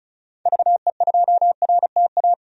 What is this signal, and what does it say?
Morse code, heard as a keyed tone, about 36 wpm, 700 Hz. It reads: VE2RTA
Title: DXCC-100 Call Signs - 6 Characters - 35wpm